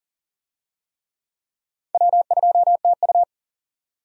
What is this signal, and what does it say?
Morse code, heard as a keyed tone, about 37 wpm, 700 Hz. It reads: W2TU
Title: USA Call Signs - 4 Characters - 40wpm